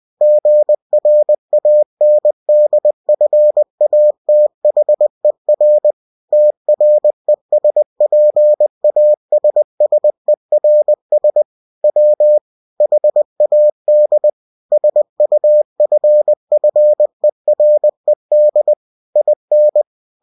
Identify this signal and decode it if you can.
Morse code, heard as a keyed tone, about 20 wpm, 600 Hz. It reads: GRANDFATHER TRESPASSERS W HAD SUFFERED IN
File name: cw-books / winnie-the-pooh / winnie-20wpm-page-0022